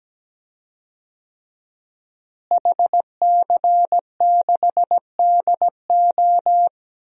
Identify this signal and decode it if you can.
Morse code, heard as a keyed tone, about 17 wpm, 700 Hz. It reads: HC6DO
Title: DXCC-100 Call Signs - 5 Characters - Repeated 3x - 17wpm